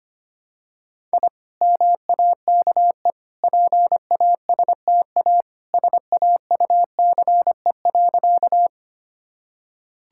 Morse code, heard as a keyed tone, about 25 wpm, 690 Hz. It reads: I MAKE PASTA SAUCE.
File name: Easy Sentences with Hints - 25wpm